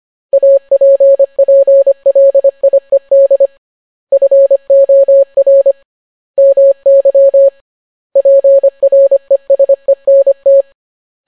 Morse code, heard as a keyed tone, about 25 wpm, 550 Hz. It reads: APPLIED FOR MY PRESENT